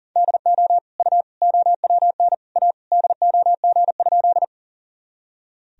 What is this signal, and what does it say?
Morse code, heard as a keyed tone, about 40 wpm, 700 Hz. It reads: DO U OWN A DOG?
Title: Sentences from Top 400 Words and Abbreviations 40wpm